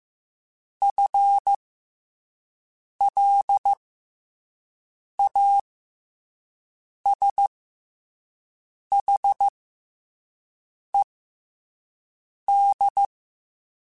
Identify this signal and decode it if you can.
Morse code, heard as a keyed tone, about 15 wpm, spaced as for 5 wpm, 800 Hz. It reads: FLASHED
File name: FL Runs / FL Runs mp3s / FL Runs @5